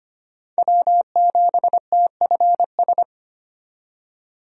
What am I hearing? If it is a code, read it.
Morse code, heard as a keyed tone, about 25 wpm, 700 Hz. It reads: W7TFS
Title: USA Call Signs - 5 Characters - 25wpm